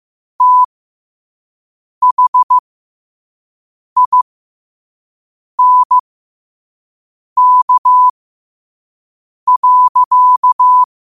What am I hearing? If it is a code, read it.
Morse code, heard as a keyed tone, about 15 wpm, spaced as for 5 wpm, 1000 Hz. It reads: THINK.
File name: qotdfiles / 5wpm